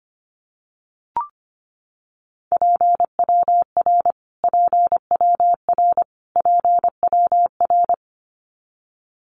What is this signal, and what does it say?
Morse code, heard as a keyed tone, about 25 wpm, 700 Hz. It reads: PWR PWR PWR
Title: Abbreviations Repeated 3x - 25wpm